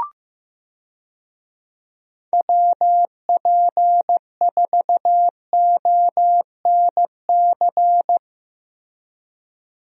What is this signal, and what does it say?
Morse code, heard as a keyed tone, about 15 wpm, 700 Hz. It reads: WP4ONC
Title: USA Call Signs - 6 Characters - Repeated 3x - 15wpm